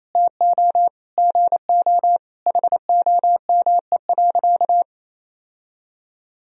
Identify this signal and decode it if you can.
Morse code, heard as a keyed tone, about 28 wpm, 700 Hz. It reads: TO GO HOME.
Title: Sentences from Top 1000 Words - Review Part 3 - 28wpm